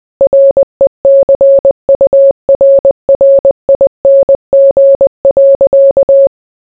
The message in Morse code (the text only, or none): RECURRING.